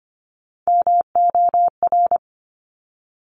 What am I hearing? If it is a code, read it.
Morse code, heard as a keyed tone, about 25 wpm, 700 Hz. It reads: MOR
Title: Top Three Letter Combinations in Words 25wpm